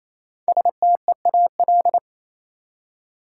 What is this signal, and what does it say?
Morse code, heard as a keyed tone, about 28 wpm, 700 Hz. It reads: STEAL
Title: Five Letter Words - 28wpm